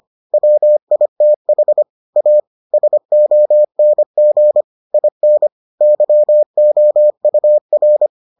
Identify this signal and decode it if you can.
Morse code, heard as a keyed tone, about 25 wpm, 600 Hz. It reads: WITH A SONG IN YOUR